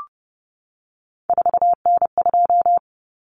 Morse code, heard as a keyed tone, about 30 wpm, 700 Hz. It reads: 4N2
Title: Common Cut Numbers - 3-Digit 30wpm